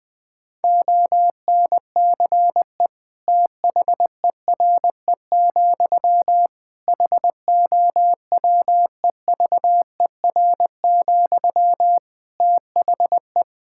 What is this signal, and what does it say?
Morse code, heard as a keyed tone, about 21 wpm, 700 Hz. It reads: ONCE THERE, HOWEVER, THE
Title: Sentences from Top 1000 Words - Review Part 6 - 20wpm